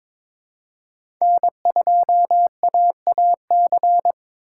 Morse code, heard as a keyed tone, about 22 wpm, 700 Hz. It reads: N2AAC
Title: USA Call Signs - 5 Characters - 2x Character Spacing - 22wpm